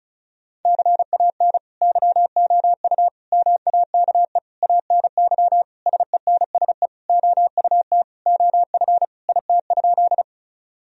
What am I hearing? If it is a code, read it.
Morse code, heard as a keyed tone, about 36 wpm, 700 Hz. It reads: CAN YOU MAKE ANY SENSE OUT OF IT?
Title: Sentences from Top 1000 Words - Review Part 4 - 35wpm